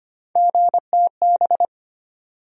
Morse code, heard as a keyed tone, about 25 wpm, 700 Hz. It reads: GTB
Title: USA Call Sign Postfixes - 3 Letters 25wpm